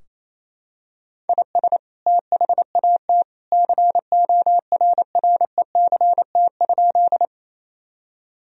Morse code, heard as a keyed tone, about 28 wpm, 700 Hz. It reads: IS THAT CORRECT?
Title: Sentences from Top 1000 Words - Review Part 2 - 28wpm